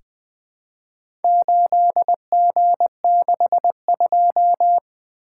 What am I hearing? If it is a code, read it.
Morse code, heard as a keyed tone, about 20 wpm, 700 Hz. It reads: 8G62
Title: Four Letter-Number Combinations - 20wpm